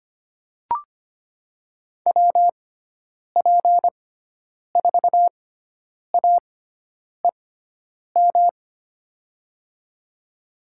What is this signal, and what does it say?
Morse code, heard as a keyed tone, about 25 wpm, spaced as for 9 wpm, 700 Hz. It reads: WP4AEM
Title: USA Call Signs - 6 Characters - 6x Character Spacing - 25wpm